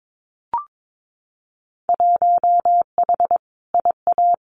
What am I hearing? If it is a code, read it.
Morse code, heard as a keyed tone, about 22 wpm, 700 Hz. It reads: 1H IA
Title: Winter Field Day Exchanges 22wpm